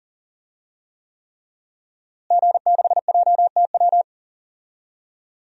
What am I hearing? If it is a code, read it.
Morse code, heard as a keyed tone, about 40 wpm, 700 Hz. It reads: G6JTW
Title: DXCC-100 Call Signs - 5 Characters - Repeated 3x - 40wpm